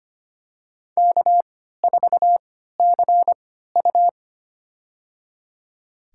Morse code, heard as a keyed tone, about 25 wpm, spaced as for 14 wpm, 700 Hz. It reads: K4CU
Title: USA Call Signs - 4 Characters - 3x Character Spacing - 25wpm